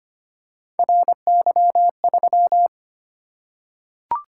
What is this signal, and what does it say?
Morse code, heard as a keyed tone, about 25 wpm, 700 Hz. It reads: RY3
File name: Three Letter-Number Combinations - 25wpm